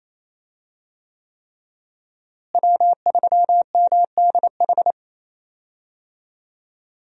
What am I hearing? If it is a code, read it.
Morse code, heard as a keyed tone, about 28 wpm, 700 Hz. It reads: W3MDH
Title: USA Call Signs - 5 Characters - Repeated 3x - 28wpm